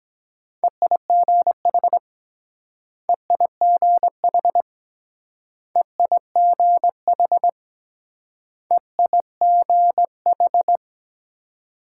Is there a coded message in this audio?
Yes: Morse code from 0.6 to 10.8 s, heard as a keyed tone, about 21 wpm, 700 Hz. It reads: EIGH EIGH EIGH EIGH